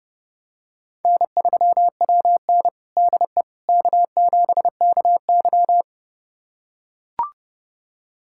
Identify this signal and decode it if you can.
Morse code, heard as a keyed tone, about 30 wpm, 700 Hz. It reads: N3WN DE K7KY
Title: USA Callsign DE Callsign - 4 Characters - 30wpm